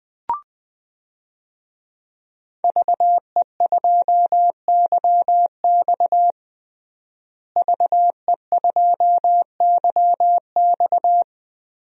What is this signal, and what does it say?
Morse code, heard as a keyed tone, about 20 wpm, 700 Hz. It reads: VE2YX VE2YX
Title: DXCC-100 Call Signs - 5 Characters - Repeated 2x - 20wpm